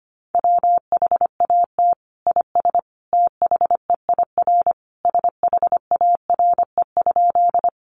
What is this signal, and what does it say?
Morse code, heard as a keyed tone, about 25 wpm, 700 Hz. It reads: WHAT IS THEIR SHARE?
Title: Sentences from Top 900 Words 25wpm